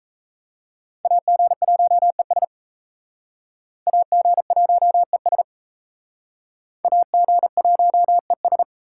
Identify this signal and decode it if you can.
Morse code, heard as a keyed tone, about 38 wpm, 700 Hz. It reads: AG1ES AG1ES AG1ES